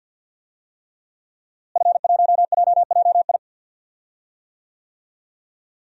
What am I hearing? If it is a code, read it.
Morse code, heard as a keyed tone, about 48 wpm, 700 Hz. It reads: F1JJI
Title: DXCC-100 Call Signs - 5 Characters - Repeated 3x - 50wpm